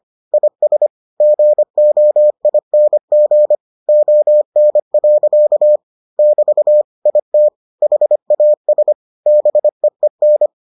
Morse code, heard as a keyed tone, about 25 wpm, 600 Hz. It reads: IS GOING ON. <BT> IT HAS BEEN